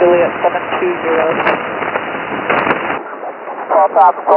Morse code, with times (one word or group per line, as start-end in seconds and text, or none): none